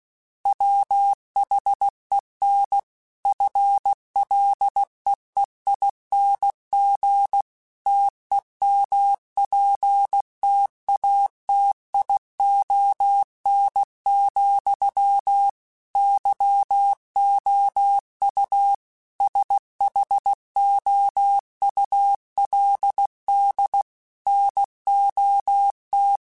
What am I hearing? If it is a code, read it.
Morse code, heard as a keyed tone, about 16 wpm, 800 Hz. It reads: WHEN FLEEING TEMPTATION, YOU SHOULD NOT
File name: Chuckles / Chuckles-mp3s / Temptation @15